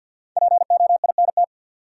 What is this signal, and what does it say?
Morse code, heard as a keyed tone, about 47 wpm, 700 Hz. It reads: POINT
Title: Sentences from Top 200 Words - New Word Drill - 50wpm